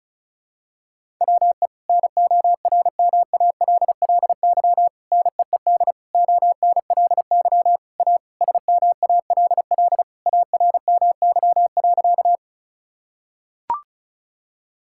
Morse code, heard as a keyed tone, about 36 wpm, 690 Hz. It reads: WE NORMALLY NEED ONLY A SMALL ARMY.
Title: Sentences from Top 1000 Words - Review Part 6 - 35wpm